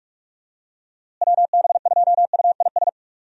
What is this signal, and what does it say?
Morse code, heard as a keyed tone, about 43 wpm, 700 Hz. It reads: WB2UIS
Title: USA Call Signs - 6 Characters - Repeated 2x - 45wpm